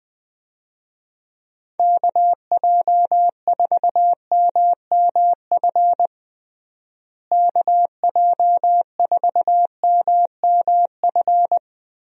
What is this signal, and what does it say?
Morse code, heard as a keyed tone, about 20 wpm, 700 Hz. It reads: KJ4MMF KJ4MMF
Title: USA Call Signs - 6 Characters - Repeated 3x - 20wpm